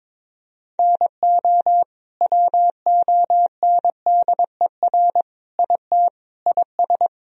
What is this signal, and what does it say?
Morse code, heard as a keyed tone, about 22 wpm, 700 Hz. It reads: NO WONDER IT IS
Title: Sentences from Top 1000 Words - Review Part 3 - 22wpm